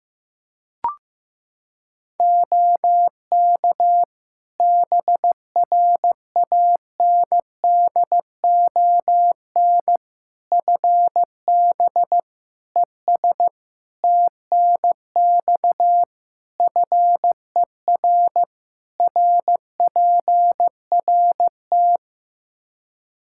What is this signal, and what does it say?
Morse code, heard as a keyed tone, about 15 wpm, 700 Hz. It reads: OK BRANDON FB ES TNX FER RPRT
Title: K1USN SST QSO Practice 15wpm